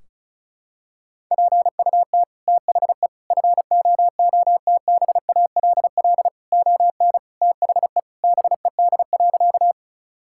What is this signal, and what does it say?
Morse code, heard as a keyed tone, about 34 wpm, 700 Hz. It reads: PUT THE FOOTBALL ON THE BED.
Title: Sentences from Top 1600 Words 35wpm